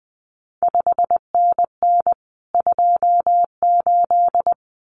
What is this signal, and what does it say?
Morse code, heard as a keyed tone, about 20 wpm, 700 Hz. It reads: 5NN 28
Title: CQ WW DX Contest Exchange Encore - 20wpm